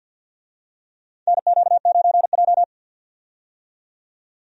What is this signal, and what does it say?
Morse code, heard as a keyed tone, about 50 wpm, 700 Hz. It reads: NQ9J
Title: USA Call Signs - 4 Characters - 50wpm